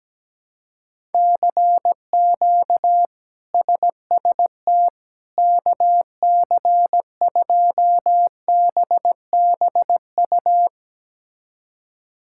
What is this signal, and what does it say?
Morse code, heard as a keyed tone, about 17 wpm, 700 Hz. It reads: CQ SST KC2BBU
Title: K1USN SST QSO Practice - 17wpm